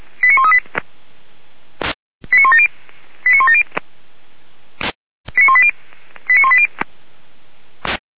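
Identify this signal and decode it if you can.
Five-tone selective call (ZVEI), heard as quick runs of five tones, tones of about 70 ms. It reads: 89138 89170 89170 89180 89180
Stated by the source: ZVEI five-tone call received on SDR, from the Feuerwehr Orlamünde fire brigade audio archive.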